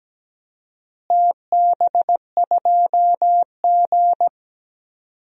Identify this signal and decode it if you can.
Morse code, heard as a keyed tone, about 17 wpm, 700 Hz. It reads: TB2G
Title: DXCC-100 Call Signs - 4 Characters - Repeated 2x - 17wpm